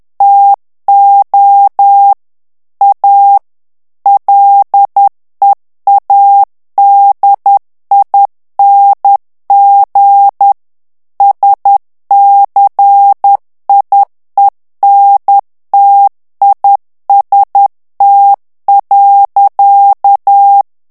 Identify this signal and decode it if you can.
Morse code, heard as a keyed tone, about 10 wpm, 800 Hz. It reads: TO A LEADING SCIENTIST.
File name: ANGLAIS texte clair 2 vitesse 10 mots minute